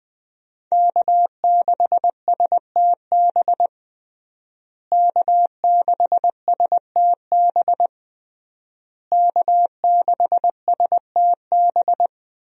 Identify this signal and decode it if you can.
Morse code, heard as a keyed tone, about 21 wpm, 700 Hz. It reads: K6STB K6STB K6STB
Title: USA Call Signs - 5 Characters - Repeated 3x - 20wpm